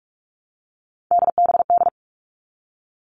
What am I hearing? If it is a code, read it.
Morse code, heard as a keyed tone, about 44 wpm, 700 Hz. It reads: DBD
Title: All Cut Numbers - 3-Digit 45wpm